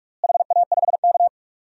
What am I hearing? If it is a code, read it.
Morse code, heard as a keyed tone, about 42 wpm, 700 Hz. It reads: HA5K
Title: DXCC-100 Call Signs - 4 Characters - Repeated 2x - 45wpm